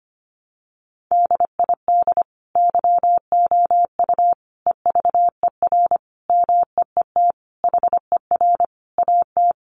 Morse code, heard as a keyed tone, about 25 wpm, 700 Hz. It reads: DID YOU EVER MEET HER AT
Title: Sentences from Top 1100 Words 25wpm